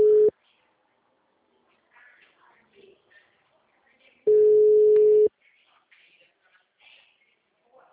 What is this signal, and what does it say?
Ringback tone (Europe, 425 Hz), with speech.